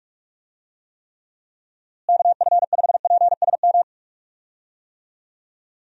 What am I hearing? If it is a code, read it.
Morse code, heard as a keyed tone, about 41 wpm, 700 Hz. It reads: KF5PSM
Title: USA Call Signs - 6 Characters - Repeated 2x - 45wpm